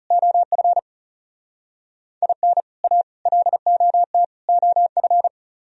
Morse code, heard as a keyed tone, about 36 wpm, 690 Hz. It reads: OF IN A LOT OF
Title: Speed-Racing - Sets of 4 Words 30wpm